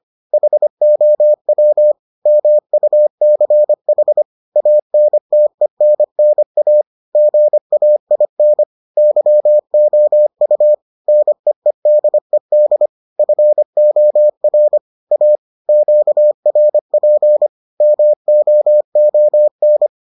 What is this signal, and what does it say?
Morse code, heard as a keyed tone, about 25 wpm, 600 Hz. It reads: HOW MUCH ANTENNA GAIN YOU NEEDED FOR A QRP MOON